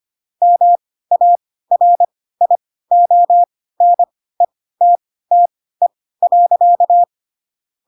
Morse code, heard as a keyed tone, about 25 wpm, spaced as for 16 wpm, 700 Hz. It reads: MARIONETTE.